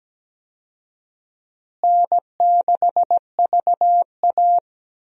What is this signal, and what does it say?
Morse code, heard as a keyed tone, about 17 wpm, 700 Hz. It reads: N6VA
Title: USA Call Signs - 4 Characters - Repeated 3x - 17wpm